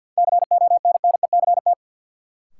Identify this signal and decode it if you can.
Morse code, heard as a keyed tone, about 50 wpm, 700 Hz. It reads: CONNECT
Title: Sentences from Top 1500 Words - New Word Drill Alt - 50wpm